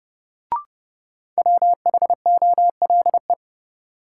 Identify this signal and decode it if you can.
Morse code, heard as a keyed tone, about 30 wpm, 700 Hz. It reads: WHOLE
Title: Sets of 2 Words - New Word Drill - 30wpm